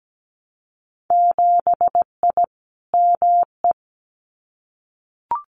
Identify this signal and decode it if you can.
Morse code, heard as a keyed tone, about 17 wpm, 700 Hz. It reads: 7I ME
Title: Winter Field Day Exchanges Encore 17wpm